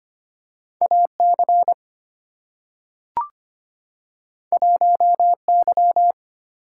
Morse code, heard as a keyed tone, about 25 wpm, 700 Hz. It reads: AC 1Y
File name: Two Letter-Number Combinations - 25wpm